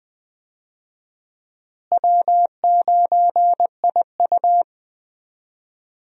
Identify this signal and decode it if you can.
Morse code, heard as a keyed tone, about 21 wpm, 700 Hz. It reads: W9IU
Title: USA Call Signs - 4 Characters - Repeated 2x - 20wpm